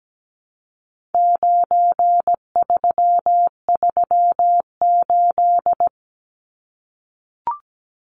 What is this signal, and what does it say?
Morse code, heard as a keyed tone, about 17 wpm, 700 Hz. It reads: 9338
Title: Numbers - 4-Digit 17wpm